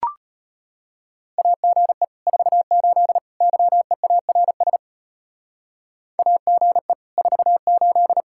Morse code, heard as a keyed tone, about 30 wpm, 700 Hz. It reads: AGE 48 YEARS AGE 48
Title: Speed-Racing Common QSO Phrases Encore - 25wpm